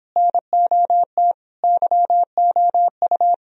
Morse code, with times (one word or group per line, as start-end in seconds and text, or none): 0.2-1.3 NOT
1.6-3.3 YOU